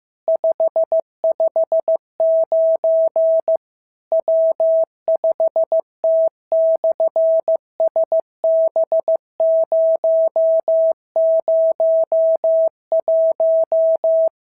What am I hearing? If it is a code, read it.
Morse code, heard as a keyed tone, about 15 wpm, 650 Hz. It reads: 559 W5T/SB001
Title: SOTA QSO Practice 15wpm